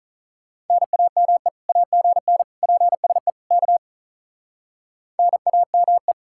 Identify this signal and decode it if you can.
Morse code, heard as a keyed tone, about 43 wpm, 700 Hz. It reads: NAME AGN PSE K NAME
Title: Speed-Racing Common QSO Phrases - 35wpm